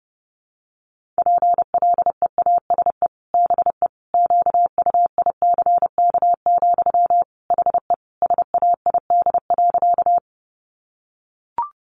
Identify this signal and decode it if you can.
Morse code, heard as a keyed tone, about 30 wpm, 700 Hz. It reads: PLEASE BE QUICK, HE SAID.
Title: Sentences from Top 1500 Words 30wpm